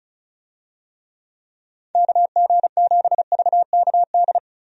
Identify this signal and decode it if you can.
Morse code, heard as a keyed tone, about 36 wpm, 700 Hz. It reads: KG7VKD
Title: USA Call Signs - 6 Characters - 35wpm